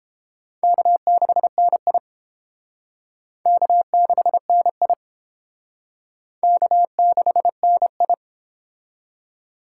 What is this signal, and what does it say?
Morse code, heard as a keyed tone, about 30 wpm, 700 Hz. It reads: K6NI K6NI K6NI